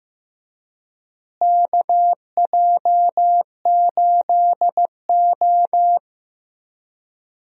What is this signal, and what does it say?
Morse code, heard as a keyed tone, about 15 wpm, 700 Hz. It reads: KJ8O
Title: USA Call Signs - 4 Characters - Repeated 2x - 15wpm